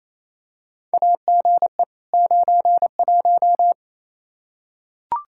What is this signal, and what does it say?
Morse code, heard as a keyed tone, about 28 wpm, 700 Hz. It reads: AGE 91
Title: Practice QSO - Exchange 3 - More Info - 28wpm